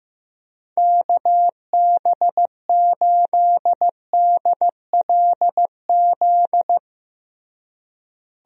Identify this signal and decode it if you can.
Morse code, heard as a keyed tone, about 15 wpm, 700 Hz. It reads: KB8DLZ
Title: USA Call Signs - 6 Characters - Repeated 3x - 15wpm